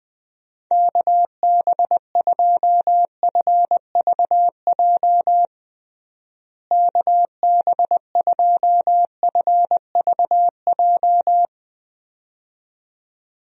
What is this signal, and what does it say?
Morse code, heard as a keyed tone, about 20 wpm, 700 Hz. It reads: KB2FVJ KB2FVJ
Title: USA Call Signs - 6 Characters - Repeated 2x - 20wpm